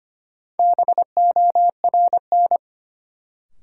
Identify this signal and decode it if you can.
Morse code, heard as a keyed tone, about 25 wpm, 700 Hz. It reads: BORN
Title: Sentences from Top 1000 Words - New Word Drill - 25wpm